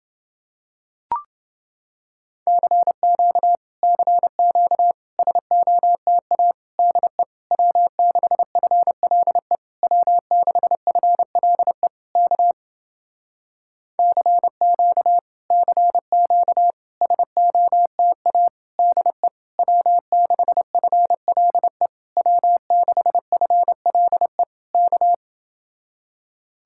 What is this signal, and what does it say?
Morse code, heard as a keyed tone, about 31 wpm, 700 Hz. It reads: CQ CQ SOTA DE W6FLE W6FLE K CQ CQ SOTA DE W6FLE W6FLE K